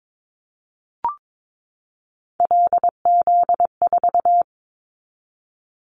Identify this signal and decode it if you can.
Morse code, heard as a keyed tone, about 22 wpm, 700 Hz. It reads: LZ4
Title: Callsign Prefixes for Top 100 DXCC - 22wpm